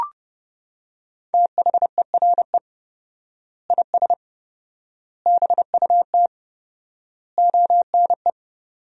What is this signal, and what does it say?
Morse code, heard as a keyed tone, about 32 wpm, 700 Hz. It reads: THERE IS BUT ONE
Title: 4x Word-Spacing - Sentences from Top 100 Words - 30wpm